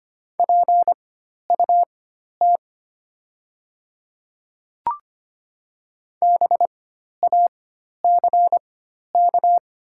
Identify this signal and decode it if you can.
Morse code, heard as a keyed tone, about 25 wpm, spaced as for 12 wpm, 690 Hz. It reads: PUT BACK
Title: 4x Character Spacing - Top 500 Words - 25wpm